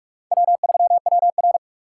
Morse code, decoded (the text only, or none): W3WR